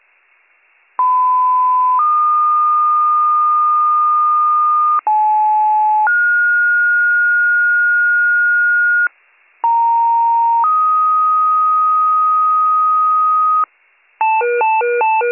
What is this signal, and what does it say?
Two-tone page, heard as a one-second tone, then a three-second tone, about 1010 then 1220 Hz, about 860 then 1430 Hz, about 930 then 1250 Hz. It is heard from 1.0 to 13.6 s.